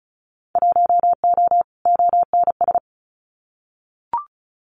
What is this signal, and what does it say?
Morse code, heard as a keyed tone, about 34 wpm, 700 Hz. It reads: 1O ONS
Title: Winter Field Day Exchanges Encore 35wpm